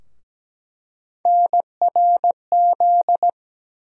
Morse code, heard as a keyed tone, about 17 wpm, 700 Hz. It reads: NRZ
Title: USA Call Sign Postfixes - 3 Letters 17wpm